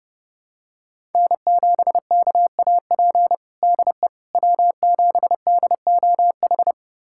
Morse code, heard as a keyed tone, about 32 wpm, 700 Hz. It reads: N7KAP DE W7DOH
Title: USA Callsign DE Callsign - 5 Characters - 30wpm